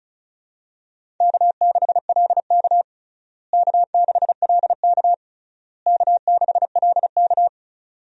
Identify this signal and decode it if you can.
Morse code, heard as a keyed tone, about 34 wpm, 700 Hz. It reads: K6LK K6LK K6LK
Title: USA Call Signs - 4 Characters - Repeated 3x - 35wpm